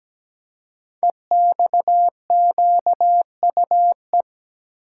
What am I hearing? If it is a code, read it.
Morse code, heard as a keyed tone, about 17 wpm, 700 Hz. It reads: EXQUE
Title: Code Groups - 5 Letter:Numbers - 17wpm